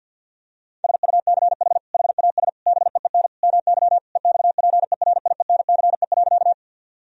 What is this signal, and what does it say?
Morse code, heard as a keyed tone, about 50 wpm, 700 Hz. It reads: SUCH HAS BEEN MY EXPERIENCE.
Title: Sentences from Top 1000 Words - Review Part 3 - 50wpm